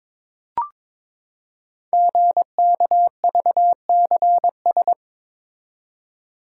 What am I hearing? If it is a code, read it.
Morse code, heard as a keyed tone, about 22 wpm, 700 Hz. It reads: GKVCS
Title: Code Groups - 5 Letters - 22wpm